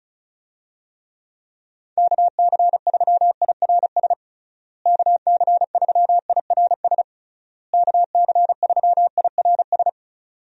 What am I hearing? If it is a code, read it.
Morse code, heard as a keyed tone, about 36 wpm, 700 Hz. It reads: KC3IRS KC3IRS KC3IRS
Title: USA Call Signs - 6 Characters - Repeated 3x - 35wpm